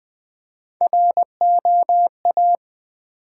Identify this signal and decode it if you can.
Morse code, heard as a keyed tone, about 20 wpm, 700 Hz. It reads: ROA
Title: Build A Word - 1 Letter At A Time - 4-Letter Words - 20wpm